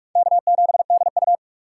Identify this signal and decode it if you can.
Morse code, heard as a keyed tone, about 46 wpm, 700 Hz. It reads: K7DU
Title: USA Call Signs - 4 Characters - 3x Character Spacing - 45wpm